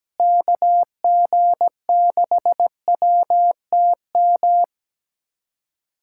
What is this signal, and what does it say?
Morse code, heard as a keyed tone, about 17 wpm, 700 Hz. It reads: KG6WTM
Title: USA Callsign DE Callsign 17wpm